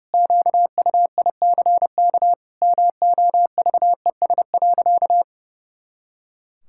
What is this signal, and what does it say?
Morse code, heard as a keyed tone, about 29 wpm, 700 Hz. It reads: QUICK MOVES.
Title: Pangram Challenge v2 - 30wpm